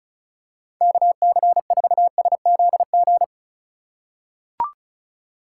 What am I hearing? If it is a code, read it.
Morse code, heard as a keyed tone, about 36 wpm, 700 Hz. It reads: KC4SZG
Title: USA Call Signs - 6 Characters - 2x Character Spacing - 35wpm